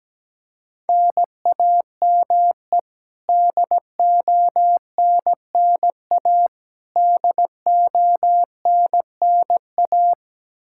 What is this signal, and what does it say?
Morse code, heard as a keyed tone, about 17 wpm, 700 Hz. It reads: NAME DONNA DONNA
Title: Practice QSO - Exchange 1 - Beginning - 17wpm